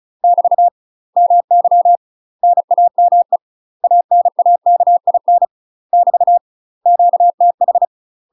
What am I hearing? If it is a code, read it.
Morse code, heard as a keyed tone, about 36 wpm, 700 Hz. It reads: <BT> MY NAME ANAKIN <BT> QTH
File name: UK MASTER QSOs TRAINING / 35WPM / UK MASTER QSOs 0013